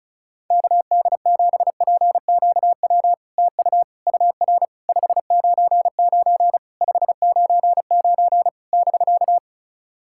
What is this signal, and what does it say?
Morse code, heard as a keyed tone, about 36 wpm, 700 Hz. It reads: KD7PQW TU UR 599 599 <BK>
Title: SOTA QSO Practice 35wpm